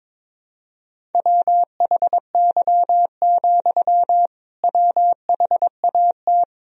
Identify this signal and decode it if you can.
Morse code, heard as a keyed tone, about 22 wpm, 700 Hz. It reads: WHY, WHAT
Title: Sentences from Top 200 Words and Abbreviations 22wpm